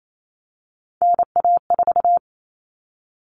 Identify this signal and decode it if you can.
Morse code, heard as a keyed tone, about 28 wpm, 700 Hz. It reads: NA4
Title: All Cut Numbers - 3-Digit 28wpm